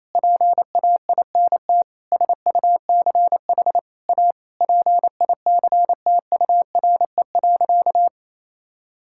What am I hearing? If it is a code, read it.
Morse code, heard as a keyed tone, about 28 wpm, 700 Hz. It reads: PAINT SUCH A PICTURE.
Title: Sentences from Top 1000 Words - Review Part 6 - 28wpm